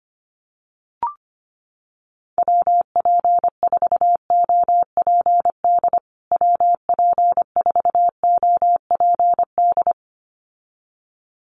Morse code, heard as a keyed tone, about 25 wpm, 700 Hz. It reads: WP4OPD WP4OPD
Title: USA Call Signs Repeated 2x 25wpm